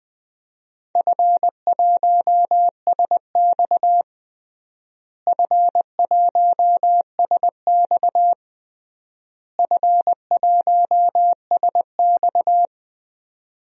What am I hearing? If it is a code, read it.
Morse code, heard as a keyed tone, about 21 wpm, 700 Hz. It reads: F1SX F1SX F1SX
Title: DXCC-100 Call Signs - 4 Characters - Repeated 3x - 20wpm